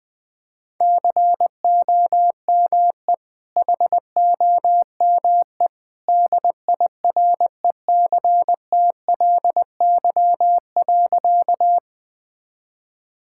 Morse code, heard as a keyed tone, about 20 wpm, 700 Hz. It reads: COME HOME DIRECTLY.
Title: Sentences from Top 1000 Words - Review Part 5 - 20wpm